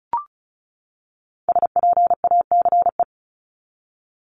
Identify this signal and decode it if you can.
Morse code, heard as a keyed tone, about 34 wpm, 700 Hz. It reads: SPACE
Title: Top 501-750 Words 35wpm